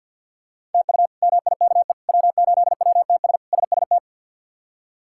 Morse code, heard as a keyed tone, about 52 wpm, 700 Hz. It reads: TU MIKE W8WTS SST
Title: K1USN SST QSO Practice - 50wpm